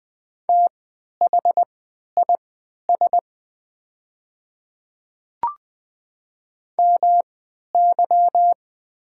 Morse code, heard as a keyed tone, about 20 wpm, spaced as for 11 wpm, 690 Hz. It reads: THIS MY
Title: 3x Character Spacing - Top 200 Words - 20wpm